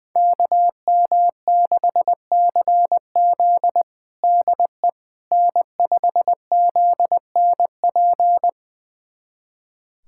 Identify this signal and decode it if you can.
Morse code, heard as a keyed tone, about 21 wpm, 700 Hz. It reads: KM6CZ DE N5ZNP
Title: USA Callsign DE Callsign - 5 Characters - 20wpm